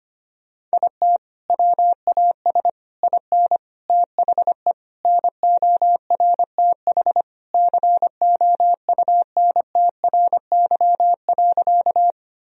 Morse code, heard as a keyed tone, about 25 wpm, 700 Hz. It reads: IT WAS IN THE NORTH COUNTRY.
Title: Sentences from Top 1000 Words - Review Part 1 - 25wpm